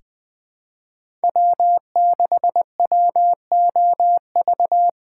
Morse code, heard as a keyed tone, about 20 wpm, 700 Hz. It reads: W6WOV
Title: USA Call Signs - 5 Characters - 3x Character Spacing - 20wpm